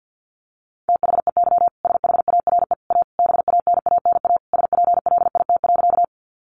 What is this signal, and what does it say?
Morse code, heard as a keyed tone, about 50 wpm, 700 Hz. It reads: THEY SHARE A BANANA SPLIT.
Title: Additional Easy Phrases 50wpm